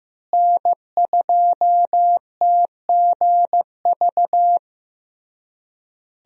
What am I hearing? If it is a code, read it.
Morse code, heard as a keyed tone, about 15 wpm, 700 Hz. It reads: N2TGV
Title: USA Call Signs - 5 Characters - Repeated 3x - 15wpm